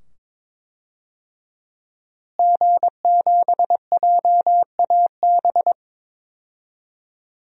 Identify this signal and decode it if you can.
Morse code, heard as a keyed tone, about 21 wpm, 700 Hz. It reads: G7JAB
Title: DXCC-100 Call Signs - 5 Characters - Repeated 2x - 22wpm